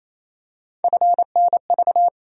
Morse code, heard as a keyed tone, about 28 wpm, 700 Hz. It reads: FNV
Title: USA Call Sign Postfixes - 3 Letters 28wpm